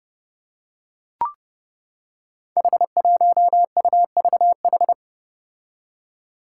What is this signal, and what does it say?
Morse code, heard as a keyed tone, about 29 wpm, 700 Hz. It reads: H1UVH
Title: Five Challenging Letter-Number Combinations 30wpm